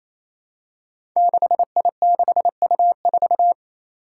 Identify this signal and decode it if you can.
Morse code, heard as a keyed tone, about 27 wpm, 700 Hz. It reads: 6I6U4
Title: Five Challenging Letter-Number Combinations 28wpm